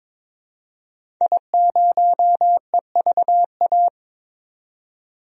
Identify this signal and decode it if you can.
Morse code, heard as a keyed tone, about 21 wpm, 700 Hz. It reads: I0EVA
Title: DXCC-100 Call Signs - 5 Characters - 22wpm